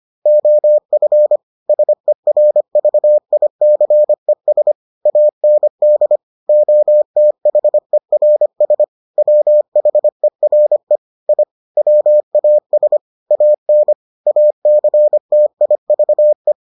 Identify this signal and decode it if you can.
Morse code, heard as a keyed tone, about 25 wpm, 600 Hz. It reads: OF SERVICES AND OTHERS WHERE I WAS AN ACTIVE